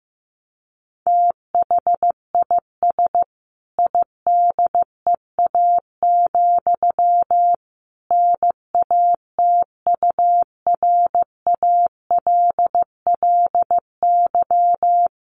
Morse code, heard as a keyed tone, about 14 wpm, 700 Hz. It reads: THIS IDEA, NATURALLY
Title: Sentences from Top 1300 Words 15wpm